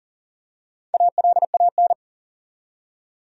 Morse code, heard as a keyed tone, about 41 wpm, 700 Hz. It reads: ALAN
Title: USA Common Names - Max 4 Letters - 40wpm